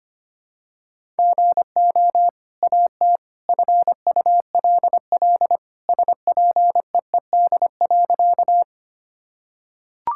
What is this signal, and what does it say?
Morse code, heard as a keyed tone, about 25 wpm, 700 Hz. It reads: GO AT FULL SPEED.
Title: Sentences from Top 1000 Words - Review Part 5 - 25wpm